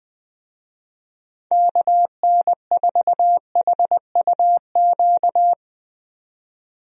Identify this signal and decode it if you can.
Morse code, heard as a keyed tone, about 21 wpm, 700 Hz. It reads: KN4HUQ
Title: USA Call Signs - 6 Characters - 20wpm